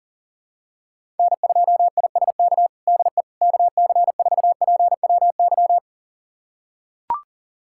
Morse code, heard as a keyed tone, about 40 wpm, 700 Hz. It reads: N2ISK DE KC4PWY
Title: USA Callsign DE Callsign 40wpm